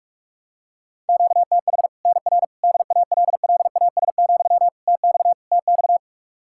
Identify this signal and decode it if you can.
Morse code, heard as a keyed tone, about 41 wpm, 700 Hz. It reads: QTH NR DALLAS, TX TX
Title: Practice QSO - Exchange 1 - Beginning - 45wpm